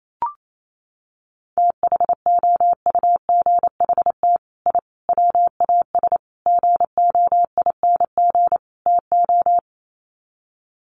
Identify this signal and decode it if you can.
Morse code, heard as a keyed tone, about 28 wpm, 700 Hz. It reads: THOUGHT I WAS GOING TO
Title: Sets of 5 Words Encore2 28wpm